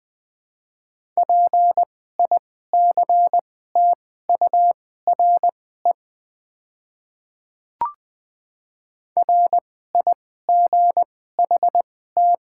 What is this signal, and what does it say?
Morse code, heard as a keyed tone, about 21 wpm, spaced as for 15 wpm, 700 Hz. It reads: PICTURE RIGHT
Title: 2x Character Spacing - Top 200 Words - 20wpm